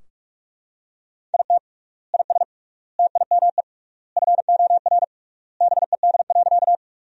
Morse code, heard as a keyed tone, about 41 wpm, 700 Hz. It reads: IT IS TIME FOR BED.
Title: 3x Word-Spacing - Sentences from Top 500 Words - 45wpm